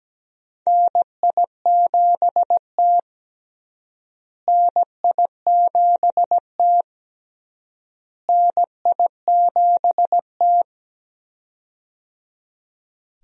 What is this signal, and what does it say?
Morse code, heard as a keyed tone, about 17 wpm, 700 Hz. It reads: NI7T NI7T NI7T
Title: USA Call Signs - 4 Characters - Repeated 3x - 17wpm